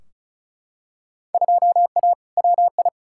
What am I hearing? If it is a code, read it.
Morse code, heard as a keyed tone, about 36 wpm, 700 Hz. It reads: 2A WI
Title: Field Day Exchanges - 35wpm